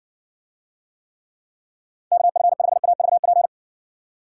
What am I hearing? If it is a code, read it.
Morse code, heard as a keyed tone, about 60 wpm, 700 Hz. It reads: KF5AVP